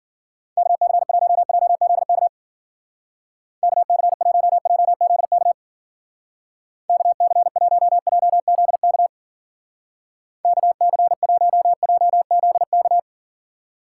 Morse code, heard as a keyed tone, about 50 wpm, 700 Hz. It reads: KC1JZK KC1JZK KC1JZK KC1JZK
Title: Speed Racing - USA Call Signs 40wpm